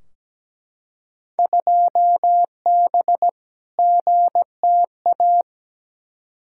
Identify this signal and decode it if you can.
Morse code, heard as a keyed tone, about 17 wpm, 700 Hz. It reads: 2B GTA
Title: Field Day Exchanges Encore - 17wpm